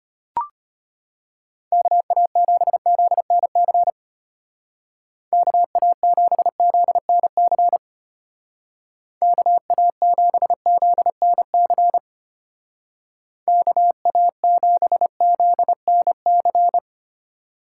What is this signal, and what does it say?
Morse code, heard as a keyed tone, about 30 wpm, 700 Hz. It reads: KA7ZNC KA7ZNC KA7ZNC KA7ZNC
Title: Speed Racing - USA 2x3 Call Signs 25wpm